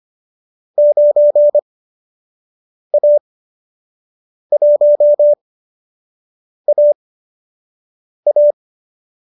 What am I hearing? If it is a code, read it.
Morse code, heard as a keyed tone, about 25 wpm, spaced as for 6 wpm, 590 Hz. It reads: 9A1AA
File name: qso012 06